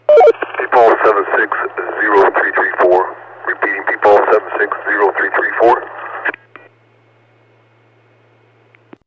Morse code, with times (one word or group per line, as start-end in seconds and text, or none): none